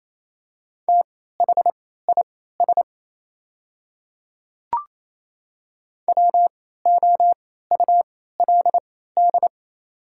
Morse code, heard as a keyed tone, about 27 wpm, spaced as for 16 wpm, 690 Hz. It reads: THIS WOULD
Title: 3x Character Spacing - Top 100 Words - 28wpm